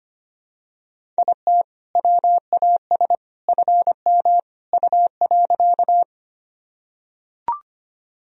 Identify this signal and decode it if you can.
Morse code, heard as a keyed tone, about 25 wpm, 700 Hz. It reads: IT WAS FM U.
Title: Sentences from Top 100 Words and Abbreviations 25wpm